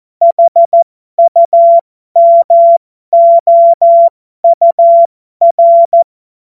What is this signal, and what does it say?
Morse code, heard as a keyed tone, about 14 wpm, spaced as for 12 wpm, 680 Hz. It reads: HUMOUR
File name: Morsekurs / Witze